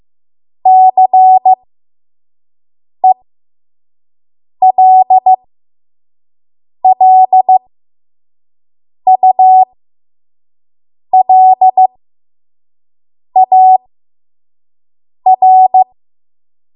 Morse code, heard as a keyed tone, about 15 wpm, spaced as for 5 wpm, 750 Hz. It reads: CELLULAR
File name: Morse / 05WPM